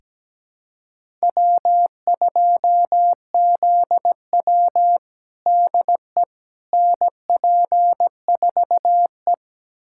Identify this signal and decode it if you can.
Morse code, heard as a keyed tone, about 17 wpm, 700 Hz. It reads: W2ZW DE NP4E
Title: USA Callsign DE Callsign - 4 Characters - 17wpm